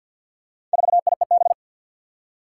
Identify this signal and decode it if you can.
Morse code, heard as a keyed tone, about 52 wpm, 700 Hz. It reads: 4IEB
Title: Four Challenging Letter-Number Combinations 50wpm